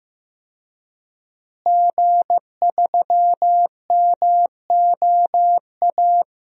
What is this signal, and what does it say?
Morse code, heard as a keyed tone, about 15 wpm, 700 Hz. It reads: G3MOA
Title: DXCC-100 Call Signs - 5 Characters - Repeated 3x - 15wpm